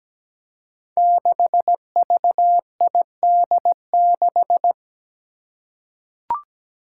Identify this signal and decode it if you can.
Morse code, heard as a keyed tone, about 17 wpm, 700 Hz. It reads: 6VID6
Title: Five Challenging Letter-Number Combinations 17wpm